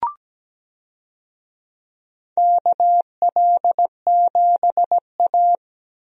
Morse code, heard as a keyed tone, about 17 wpm, 700 Hz. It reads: KL7A
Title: DXCC-100 Call Signs - 4 Characters - Repeated 3x - 17wpm